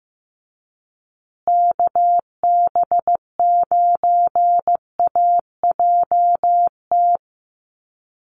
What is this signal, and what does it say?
Morse code, heard as a keyed tone, about 15 wpm, 700 Hz. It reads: KB9AJT
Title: USA Call Signs Repeated 3x 15wpm